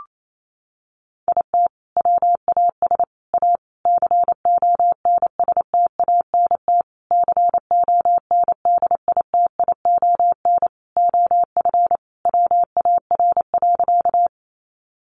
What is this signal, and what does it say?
Morse code, heard as a keyed tone, about 28 wpm, 700 Hz. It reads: IT WAS A CONSTANT CONDITION OF WAR.
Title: Sentences from Top 1500 Words 28wpm